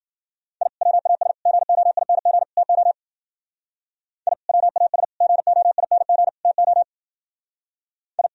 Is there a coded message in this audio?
Yes: Morse code from 0.6 to 8.3 s, heard as a keyed tone, about 51 wpm, 700 Hz. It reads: I WAS GOING TO I WAS GOING TO I